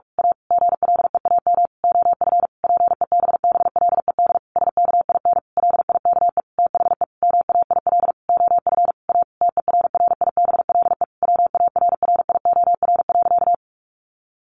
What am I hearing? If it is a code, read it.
Morse code, heard as a keyed tone, about 44 wpm, 700 Hz. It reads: A GLEAM OF PEBBLED SKIN LIKE THE MAIL OF A TERRIBLE WARRIOR.
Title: A Sound Of Thunder 45wpm